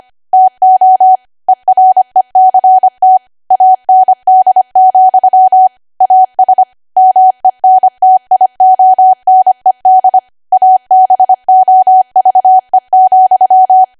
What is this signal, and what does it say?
Morse code, heard as a keyed tone, about 25 wpm, 750 Hz. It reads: TO ERECT AND, AS MENTIONED ABOVE,
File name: Archive / 25 WPM / 25WPM